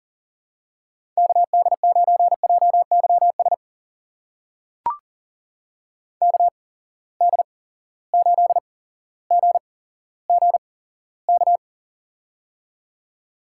Morse code, heard as a keyed tone, about 40 wpm, spaced as for 17 wpm, 700 Hz. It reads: KD9JYS KD8GGK
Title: USA Call Signs - 6 Characters - 8x Character Spacing - 40wpm